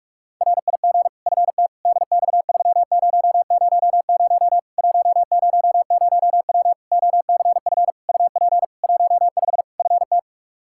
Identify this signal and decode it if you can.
Morse code, heard as a keyed tone, about 46 wpm, 700 Hz. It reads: RIG FT DX3000 100W OCF UP 15 FT
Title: K1USN SST QSO Practice 45wpm